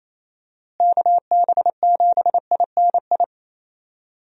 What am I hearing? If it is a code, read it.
Morse code, heard as a keyed tone, about 28 wpm, 700 Hz. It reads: KB7INI
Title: USA Call Signs - 6 Characters - Repeated 2x - 28wpm